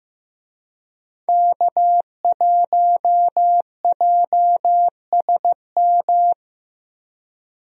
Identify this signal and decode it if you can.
Morse code, heard as a keyed tone, about 15 wpm, 700 Hz. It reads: K1JSM
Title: USA Call Signs - 5 Characters - 15wpm